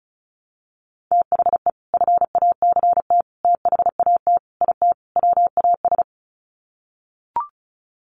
Morse code, heard as a keyed tone, about 34 wpm, 700 Hz. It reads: THE FACT THAT IT WAS
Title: Sets of 5 Words Encore2 35wpm